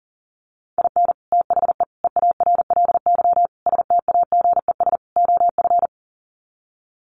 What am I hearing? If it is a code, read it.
Morse code, heard as a keyed tone, about 37 wpm, 700 Hz. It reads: IN THE EARLY STAGES OF
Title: Sets of 5 Words Encore 40wpm